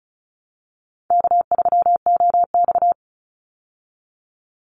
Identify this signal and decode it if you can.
Morse code, heard as a keyed tone, about 34 wpm, 700 Hz. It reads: K3OX
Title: Easy Call Signs Repeated 2x - 35wpm